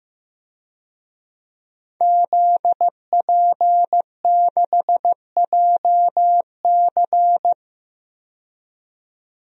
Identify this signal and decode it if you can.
Morse code, heard as a keyed tone, about 15 wpm, 700 Hz. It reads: ZP6JC
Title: DXCC-100 Call Signs - 5 Characters - 15wpm